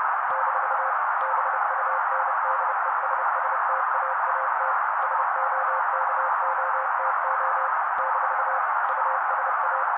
Morse code, heard as a keyed tone, about 30 wpm, 530 Hz. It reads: - BUNDESSTAAT SOKOTO - USA
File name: daily / 29wpm noise